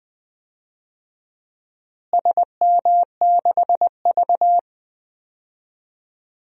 Morse code, heard as a keyed tone, about 20 wpm, 700 Hz. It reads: SM6V
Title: DXCC-100 Call Signs - 4 Characters - Repeated 3x - 20wpm